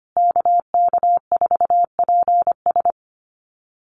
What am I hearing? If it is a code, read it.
Morse code, heard as a keyed tone, about 25 wpm, 700 Hz. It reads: KK4PS
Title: USA Call Signs Repeated 3x 25wpm